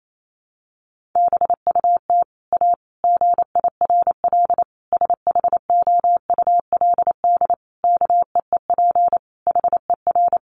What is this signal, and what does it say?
Morse code, heard as a keyed tone, about 28 wpm, 700 Hz. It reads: BUT A GIRL SHOULD KEEP HER
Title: Sentences from Top 1200 Words 28wpm